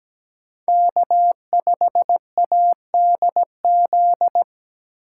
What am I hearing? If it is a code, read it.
Morse code, heard as a keyed tone, about 17 wpm, 700 Hz. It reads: K5ADZ
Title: USA Call Signs - 5 Characters - 6x Character Spacing - 17wpm